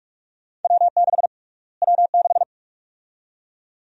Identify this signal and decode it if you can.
Morse code, heard as a keyed tone, about 46 wpm, 700 Hz. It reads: W6 W6
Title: USA Call Sign Prefixes - Repeated 2x - 45wpm